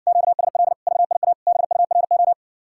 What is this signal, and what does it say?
Morse code, heard as a keyed tone, about 60 wpm, 700 Hz. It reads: QSL VIA BURO